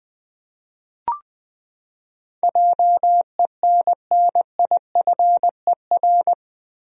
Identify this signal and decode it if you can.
Morse code, heard as a keyed tone, about 20 wpm, 700 Hz. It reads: JENNIFER
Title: USA Common Names - 20wpm